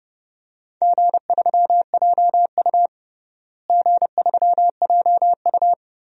Morse code, heard as a keyed tone, about 30 wpm, 700 Hz. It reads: G3JU G3JU
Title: DXCC-100 Call Signs - 4 Characters - Repeated 3x - 30wpm